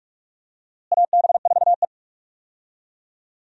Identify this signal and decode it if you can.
Morse code, heard as a keyed tone, about 42 wpm, 700 Hz. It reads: AB4E
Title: USA Call Signs - 4 Characters - Repeated 3x - 45wpm